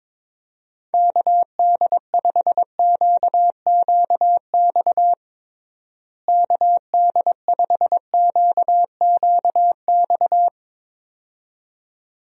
Morse code, heard as a keyed tone, about 22 wpm, 700 Hz. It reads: KD5QQX KD5QQX
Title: USA Call Signs - 6 Characters - Repeated 2x - 22wpm